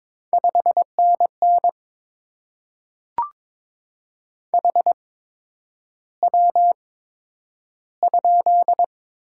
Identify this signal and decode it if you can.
Morse code, heard as a keyed tone, about 22 wpm, spaced as for 9 wpm, 700 Hz. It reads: 5NN HW?
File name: Top 20 QSO Elements - 8x Character Spacing - 22wpm